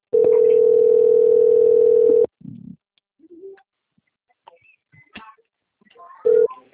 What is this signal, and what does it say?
Ringback tone (North America), 440 + 480 Hz.
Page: Forca - RJ